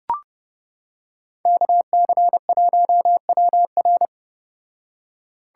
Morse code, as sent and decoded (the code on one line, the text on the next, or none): -.- -.-. .---- .-- .-.
KC1WR